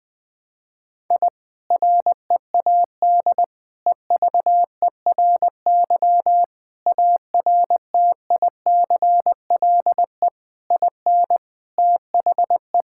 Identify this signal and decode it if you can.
Morse code, heard as a keyed tone, about 20 wpm, 700 Hz. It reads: I READ EVERY ARTICLE IN THE
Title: Sentences from Top 1000 Words - Review Part 5 - 20wpm